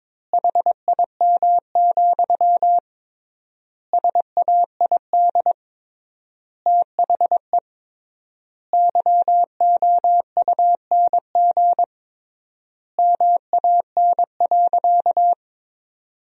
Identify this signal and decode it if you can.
Morse code, heard as a keyed tone, about 22 wpm, 700 Hz. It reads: HIM, SAID THE YOUNG MAN.
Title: 3x Word-Spacing - Sentences from Top 400 Words - 22wpm